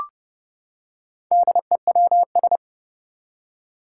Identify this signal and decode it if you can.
Morse code, heard as a keyed tone, about 29 wpm, 690 Hz. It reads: DEWS
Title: Four Challenging Letter-Number Combinations 30wpm